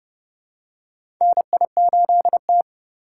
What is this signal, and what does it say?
Morse code, heard as a keyed tone, about 31 wpm, 700 Hz. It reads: NI8T
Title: USA Call Signs - 4 Characters - 3x Character Spacing - 30wpm